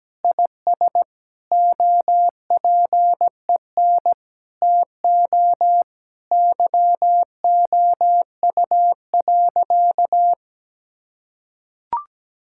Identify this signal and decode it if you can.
Morse code, heard as a keyed tone, about 17 wpm, 700 Hz. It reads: IS OPEN TO YOU.
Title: Sentences from Top 1000 Words - Review Part 6 - 17wpm